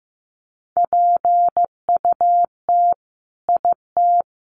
Morse code, heard as a keyed tone, about 15 wpm, 700 Hz. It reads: PUT IT
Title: Sentences from Top 200 Words 15wpm